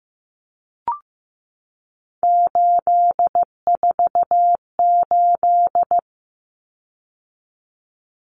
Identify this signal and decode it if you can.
Morse code, heard as a keyed tone, about 15 wpm, 700 Hz. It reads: 848
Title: Common Cut Numbers - 3-Digit 15wpm